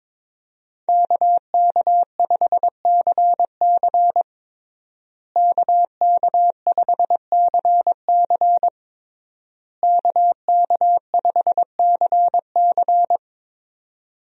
Morse code, heard as a keyed tone, about 22 wpm, 700 Hz. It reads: KK5CC KK5CC KK5CC
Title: USA Call Signs - 5 Characters - Repeated 3x - 22wpm